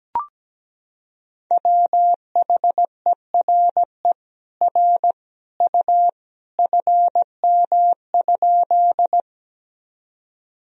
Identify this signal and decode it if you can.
Morse code, heard as a keyed tone, about 17 wpm, 700 Hz. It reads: WHERE R U FM?
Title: Sentences from Top 200 Words and Abbreviations 17wpm